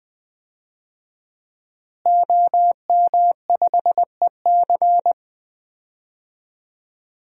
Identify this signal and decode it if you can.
Morse code, heard as a keyed tone, about 20 wpm, 700 Hz. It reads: OM5EC
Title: DXCC-100 Call Signs - 5 Characters - Repeated 3x - 20wpm